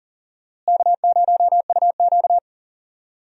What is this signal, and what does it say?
Morse code, heard as a keyed tone, about 38 wpm, 700 Hz. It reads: K0UQ
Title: USA Call Signs - 4 Characters - Repeated 3x - 40wpm